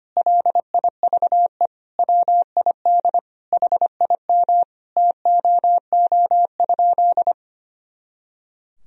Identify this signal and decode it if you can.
Morse code, heard as a keyed tone, about 25 wpm, 700 Hz. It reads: LIVE WID HIM TOO?
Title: Sentences from Top 500 Words and Abbreviations 25wpm